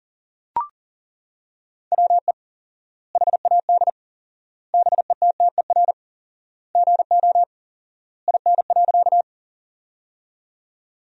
Morse code, heard as a keyed tone, about 40 wpm, 700 Hz. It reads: WE HAD BETTER GO IN.
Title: 4x Word-Spacing - Sentences from Top 400 Words - 40wpm